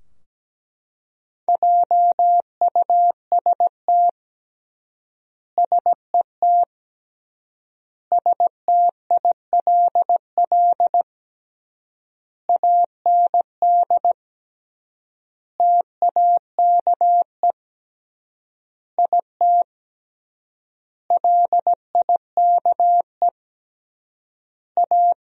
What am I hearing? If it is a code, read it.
Morse code, heard as a keyed tone, about 17 wpm, 700 Hz. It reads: JUST SET STILL AND TAKE IT LIKE A
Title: 3x Word-Spacing - Sentences from Top 300 Words - 17wpm